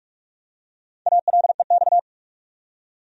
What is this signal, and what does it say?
Morse code, heard as a keyed tone, about 43 wpm, 700 Hz. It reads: ALEX
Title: USA Common Names - 45wpm